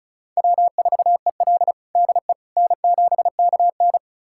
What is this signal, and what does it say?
Morse code, heard as a keyed tone, about 36 wpm, 700 Hz. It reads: W4EL DE N7KN
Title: USA Callsign DE Callsign - 4 Characters - 35wpm